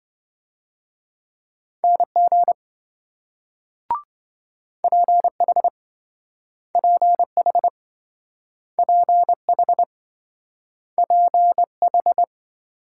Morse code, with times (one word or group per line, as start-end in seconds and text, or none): 1.8-2.5 NG
4.8-5.7 PH
6.8-7.7 PH
8.8-9.8 PH
11.0-12.2 PH